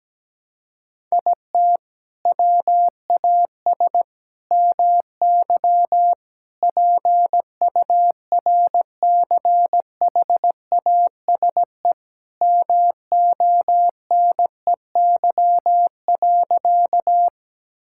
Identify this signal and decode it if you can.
Morse code, heard as a keyed tone, about 17 wpm, 700 Hz. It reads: IT WAS MY PURCHASE MONEY.
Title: Sentences from Top 1000 Words - Review Part 5 - 17wpm